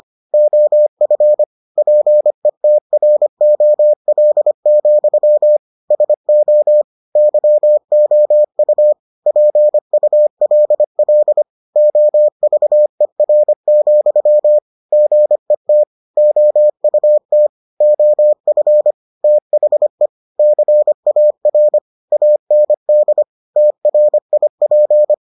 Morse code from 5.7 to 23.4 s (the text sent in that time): SO YOU PULL OVER, GET OUT OF THE CAR AND